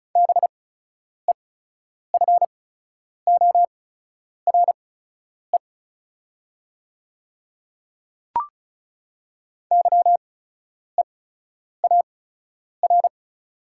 Morse code, heard as a keyed tone, about 34 wpm, spaced as for 10 wpm, 690 Hz. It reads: BEFORE YEAR
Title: 8x Character Spacing - Top 200 Words - 35wpm